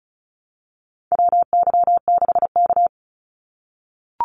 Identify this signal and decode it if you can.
Morse code, heard as a keyed tone, about 34 wpm, 700 Hz. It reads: WY6K
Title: Call Signs - 35wpm